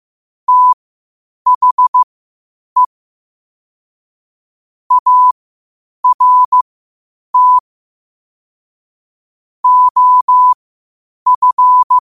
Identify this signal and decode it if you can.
Morse code, heard as a keyed tone, about 15 wpm, spaced as for 8 wpm, 1000 Hz. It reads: THE ART OF
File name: qotdfiles / 8wpm